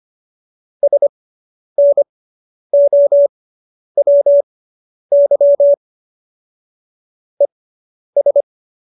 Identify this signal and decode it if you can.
Morse code, heard as a keyed tone, about 25 wpm, spaced as for 10 wpm, 570 Hz. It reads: SNOWY ES